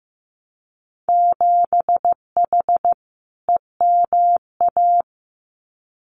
Morse code, heard as a keyed tone, about 15 wpm, 700 Hz. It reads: 7H EMA
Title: Winter Field Day Exchanges Encore 15wpm